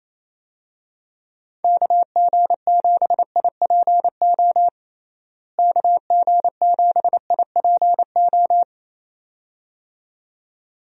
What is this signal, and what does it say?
Morse code, heard as a keyed tone, about 28 wpm, 700 Hz. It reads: KG7IPO KG7IPO
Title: USA Call Signs - 6 Characters - Repeated 2x - 28wpm